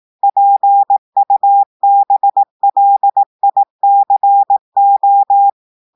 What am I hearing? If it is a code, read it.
Morse code, heard as a keyed tone, about 18 wpm, 800 Hz. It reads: PUBLICO
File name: audios / portugais / portugais texte en clair 2 vitesse 18 mots minutes